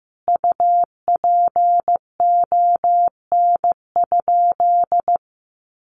Morse code, heard as a keyed tone, about 15 wpm, 700 Hz. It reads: UPON?
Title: Sentences from Top 1300 Words 15wpm